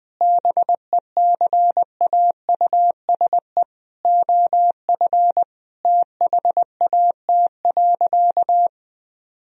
Morse code, heard as a keyed tone, about 21 wpm, 700 Hz. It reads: BECAUSE OF THAT.
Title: Sentences from Top 1000 Words - Review Part 2 - 20wpm